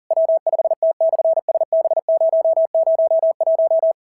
Morse code, heard as a keyed tone, about 40 wpm, 650 Hz. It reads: W5T/SB001
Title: SOTA QSO Practice 40wpm